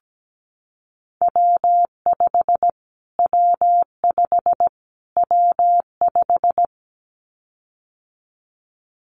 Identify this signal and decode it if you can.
Morse code, heard as a keyed tone, about 17 wpm, 700 Hz. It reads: W5 W5 W5
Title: Callsign Prefixes for Top 25 DXCC - Repeated 3x - 17wpm